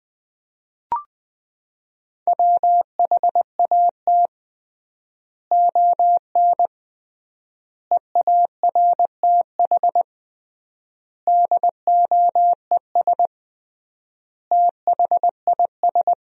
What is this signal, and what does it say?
Morse code, heard as a keyed tone, about 20 wpm, 700 Hz. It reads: WHAT ON EARTH DOES THIS
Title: 3x Word-Spacing - Sentences from Top 200 Words - 20wpm